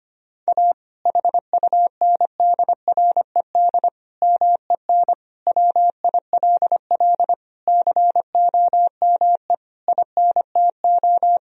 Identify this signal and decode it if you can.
Morse code, heard as a keyed tone, about 25 wpm, 700 Hz. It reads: A HUNDRED MEN WILL COME INTO
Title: Sentences from Top 1000 Words - Review Part 2 - 25wpm